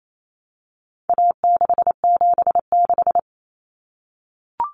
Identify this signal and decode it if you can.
Morse code, heard as a keyed tone, about 28 wpm, 700 Hz. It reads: A676
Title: Common Cut Numbers - 4-Digit 28wpm